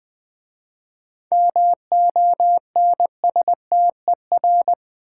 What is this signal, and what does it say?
Morse code, heard as a keyed tone, about 20 wpm, 700 Hz. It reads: MONSTER
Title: 7 Letter Words - First 3 Letters Given - 20wpm